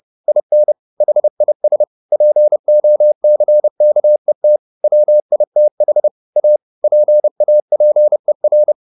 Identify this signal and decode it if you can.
Morse code, heard as a keyed tone, about 30 wpm, 600 Hz. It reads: IN HIS POCKET WITH A PAPER
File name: the old man and the sea / the old man and the sea 1 30wpm